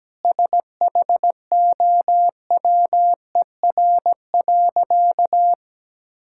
Morse code, heard as a keyed tone, about 17 wpm, 700 Hz. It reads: SHOWER.
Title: Sentences from Top 1600 Words 17wpm